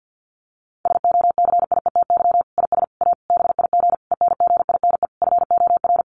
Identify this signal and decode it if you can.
Morse code, heard as a keyed tone, about 50 wpm, 700 Hz. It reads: SOCIETY IS A BIG ENGINE FOR
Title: Sentences from Top 1500 Words 50wpm